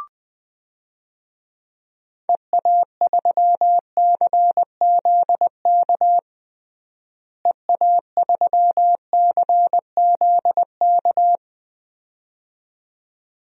Morse code, heard as a keyed tone, about 20 wpm, 700 Hz. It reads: EA3CZK EA3CZK
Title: DXCC-100 Call Signs - 6 Characters - Repeated 2x - 20wpm